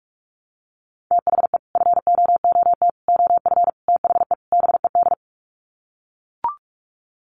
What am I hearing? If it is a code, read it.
Morse code, heard as a keyed tone, about 44 wpm, 700 Hz. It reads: THE FOOT OF THE BED
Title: Sets of 5 Words Encore2 45wpm